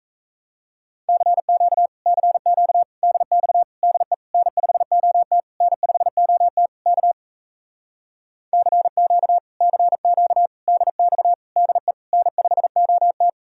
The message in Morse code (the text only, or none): CQ CQ DX DE N5OT N5OT K CQ CQ DX DE N5OT